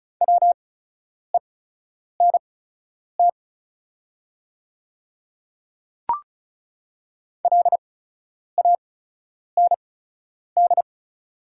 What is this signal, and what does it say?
Morse code, heard as a keyed tone, about 34 wpm, spaced as for 10 wpm, 690 Hz. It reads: WENT LAND
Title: 8x Character Spacing - Top 200 Words - 35wpm